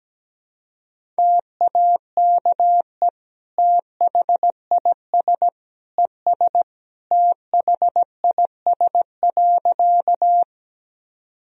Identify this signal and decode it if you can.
Morse code, heard as a keyed tone, about 17 wpm, 700 Hz. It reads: TAKE THIS ES THIS.
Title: Sentences from Top 200 Words and Abbreviations 17wpm